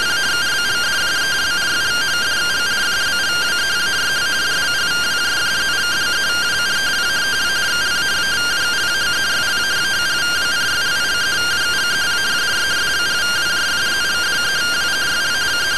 Two-tone teleprinter signal, about 170 Hz shift.